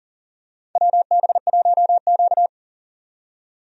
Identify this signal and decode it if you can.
Morse code, heard as a keyed tone, about 37 wpm, 700 Hz. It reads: WB1Q
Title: USA Call Signs - 4 Characters - 40wpm